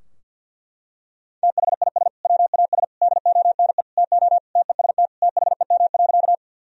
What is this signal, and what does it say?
Morse code, heard as a keyed tone, about 47 wpm, 700 Hz. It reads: THIS WAS DONE TO TEST THEM.
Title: Sentences from Top 1000 Words - Review Part 3 - 50wpm